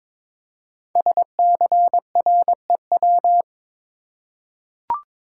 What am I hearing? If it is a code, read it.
Morse code, heard as a keyed tone, about 21 wpm, 700 Hz. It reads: SCREW
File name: Difficult Words with Hints - 22wpm